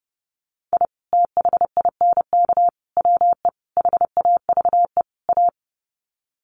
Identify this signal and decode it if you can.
Morse code, heard as a keyed tone, about 30 wpm, 700 Hz. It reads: I THINK WE HAVE A
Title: Sets of 5 Words Encore 30wpm